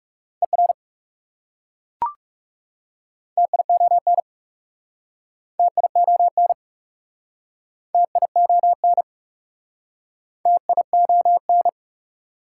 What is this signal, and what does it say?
Morse code, heard as a keyed tone, about 38 wpm, 690 Hz. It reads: ER TION TION TION TION